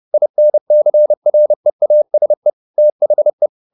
Morse code, heard as a keyed tone, about 30 wpm, 600 Hz. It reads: INCREASE THE
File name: the old man and the sea / the old man and the sea 1 30wpm